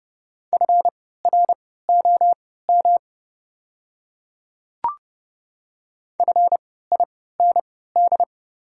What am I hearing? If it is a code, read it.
Morse code, heard as a keyed tone, about 32 wpm, spaced as for 17 wpm, 690 Hz. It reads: FROM FIND
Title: 3x Character Spacing - Top 100 Words - 30wpm